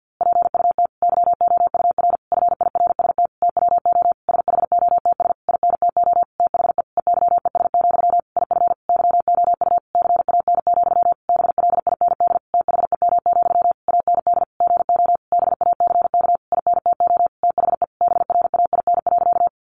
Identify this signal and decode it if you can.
Morse code, heard as a keyed tone, about 50 wpm, 700 Hz. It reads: PUT YOUR FIRST TWO SHOTS INTO THE EYES, IF YOU CAN, BLIND THEM, AND GO BACK INTO THE BRAIN.
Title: A Sound Of Thunder 50wpm